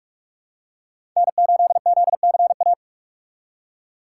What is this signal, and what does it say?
Morse code, heard as a keyed tone, about 41 wpm, 700 Hz. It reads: N8ZCA
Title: USA Call Signs - 5 Characters - 3x Character Spacing - 45wpm